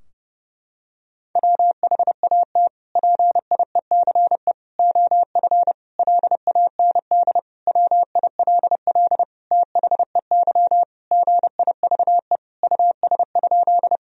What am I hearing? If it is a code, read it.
Morse code, heard as a keyed tone, about 30 wpm, 700 Hz. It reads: WHAT PIECE OF LAND WILL THEY GIVE US?
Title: Sentences from Top 1000 Words - Review Part 2 - 30wpm